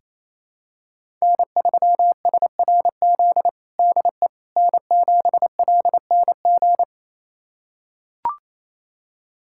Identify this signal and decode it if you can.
Morse code, heard as a keyed tone, about 28 wpm, 700 Hz. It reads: N3SRZ DE N7LNG
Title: USA Callsign DE Callsign - 5 Characters - 28wpm